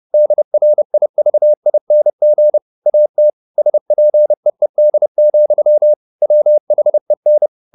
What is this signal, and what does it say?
Morse code, heard as a keyed tone, about 30 wpm, 600 Hz. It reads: DRIVING AT SPEED, WHEN
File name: the old man and the sea / the old man and the sea 1 30wpm